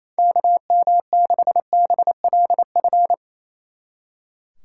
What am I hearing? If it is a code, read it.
Morse code, heard as a keyed tone, about 34 wpm, 700 Hz. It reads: KM6BLF
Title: Speed Racing - USA 2x3 Call Signs 28wpm